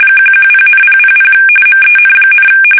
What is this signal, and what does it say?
RTTY, about 45.45 baud. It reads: 110,000 MEN,
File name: fsk850